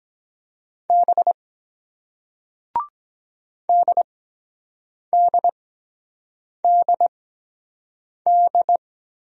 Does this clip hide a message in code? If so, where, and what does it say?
Yes: Morse code from 0.9 to 8.8 s, heard as a keyed tone, about 23 wpm, spaced as for 7 wpm, 700 Hz. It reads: B DDDD